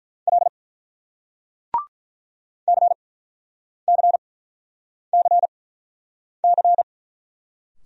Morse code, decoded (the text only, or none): L CCCC